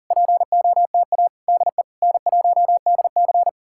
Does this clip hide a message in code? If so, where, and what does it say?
Yes: Morse code from 0.1 to 3.5 s, heard as a keyed tone, about 51 wpm, 700 Hz. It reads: POTA DE N1DC